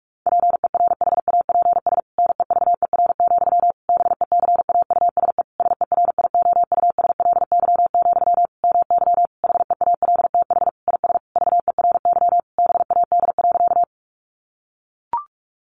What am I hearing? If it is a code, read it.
Morse code, heard as a keyed tone, about 44 wpm, 700 Hz. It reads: PERHAPS NEVER, BECAUSE SERIOUSLY, MY HEALTH IS VERY BAD.
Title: Sentences from Top 1200 Words 45wpm